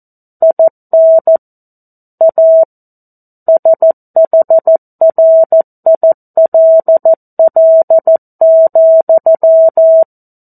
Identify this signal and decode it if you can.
Morse code, heard as a keyed tone, about 14 wpm, 650 Hz. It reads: IN A SHRILL,
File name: 02-Chapter-02 14 WPM